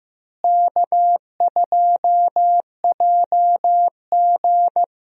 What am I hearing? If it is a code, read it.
Morse code, heard as a keyed tone, about 15 wpm, 700 Hz. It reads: K2JG
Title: USA Call Signs - 4 Characters - Repeated 2x - 15wpm